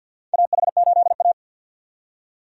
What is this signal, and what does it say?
Morse code, heard as a keyed tone, about 52 wpm, 700 Hz. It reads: AH8A
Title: USA Call Signs - 4 Characters - 50wpm